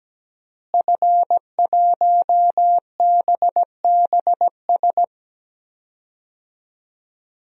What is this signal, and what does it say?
Morse code, heard as a keyed tone, about 17 wpm, 700 Hz. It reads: F1BBS
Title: DXCC-100 Call Signs - 5 Characters - Repeated 3x - 17wpm